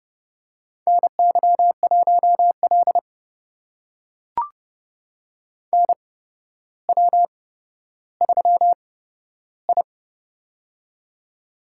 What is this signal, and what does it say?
Morse code, heard as a keyed tone, about 30 wpm, spaced as for 13 wpm, 700 Hz. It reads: NY1L NW3I
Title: USA Call Signs - 4 Characters - 8x Character Spacing - 30wpm